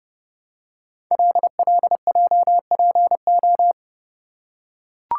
Morse code, heard as a keyed tone, about 28 wpm, 700 Hz. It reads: LLJPO
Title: Code Groups - 5 Letters - 30wpm